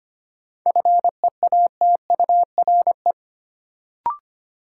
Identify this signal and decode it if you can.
Morse code, heard as a keyed tone, about 25 wpm, 700 Hz. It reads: FEATURE
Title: Sentences from Top 800 Words - New Word Drill - 25wpm